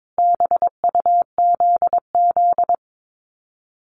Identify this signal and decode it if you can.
Morse code, heard as a keyed tone, about 22 wpm, 700 Hz. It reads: BUZZ
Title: Common QSO Phrases 22wpm